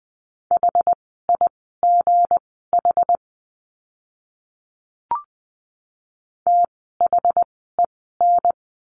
Morse code, heard as a keyed tone, about 20 wpm, spaced as for 14 wpm, 700 Hz. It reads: HIGH THEN
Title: 192 Lesson 31f - Letter v - Words 20wpm